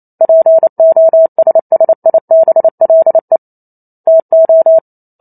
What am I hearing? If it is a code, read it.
Morse code, heard as a keyed tone, about 28 wpm, 650 Hz. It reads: POSSIBLE TO